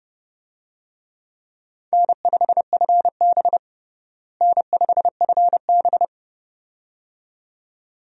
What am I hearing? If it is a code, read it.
Morse code, heard as a keyed tone, about 30 wpm, 700 Hz. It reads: N5FB N5FB
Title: USA Call Signs - 4 Characters - Repeated 2x - 30wpm